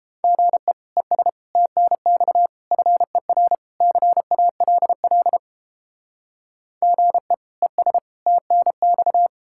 Morse code, heard as a keyed tone, about 29 wpm, 700 Hz. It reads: GE ES TNX FER CALL GE ES TNX